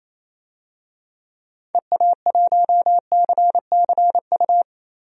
Morse code, heard as a keyed tone, about 28 wpm, 700 Hz. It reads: EA1CCU
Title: DXCC-100 Call Signs - 6 Characters - Repeated 3x - 28wpm